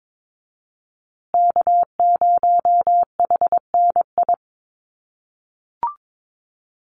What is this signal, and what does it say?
Morse code, heard as a keyed tone, about 22 wpm, 690 Hz. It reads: K0HNI
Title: USA Call Signs Repeated 3x 22wpm